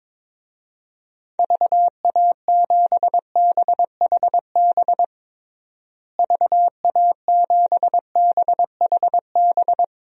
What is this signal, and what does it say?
Morse code, heard as a keyed tone, about 22 wpm, 700 Hz. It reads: VA7BHB VA7BHB
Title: DXCC-100 Call Signs - 6 Characters - Repeated 3x - 22wpm